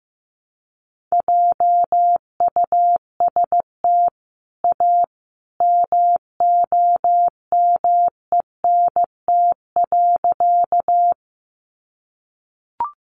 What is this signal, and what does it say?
Morse code, heard as a keyed tone, about 15 wpm, 700 Hz. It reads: JUST A MOMENT.
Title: Sentences from Top 1000 Words 15wpm